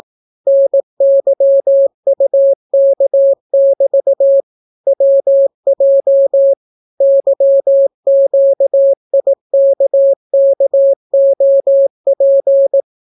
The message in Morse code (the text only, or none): NYUK<BT> WJ YQIKKOP